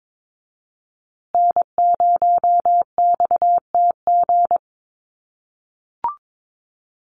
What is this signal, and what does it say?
Morse code, heard as a keyed tone, about 22 wpm, 700 Hz. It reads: N0XTG
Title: USA Call Signs Repeated 3x 22wpm